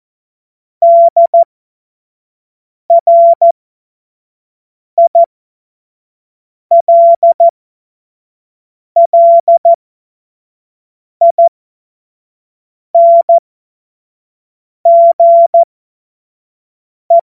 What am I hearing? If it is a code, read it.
Morse code, heard as a keyed tone, about 14 wpm, spaced as for 5 wpm, 680 Hz. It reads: DRILLINGE
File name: Morsekurs / Witze / Witz 22. Februar 2022 25 F70